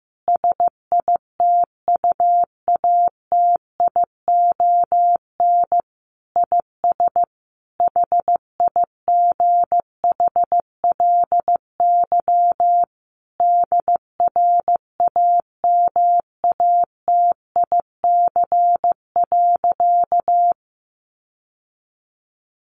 Morse code, heard as a keyed tone, about 14 wpm, 700 Hz. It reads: SITUATION IS HIGHLY DRAMATIC.
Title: Sentences from Top 1300 Words 15wpm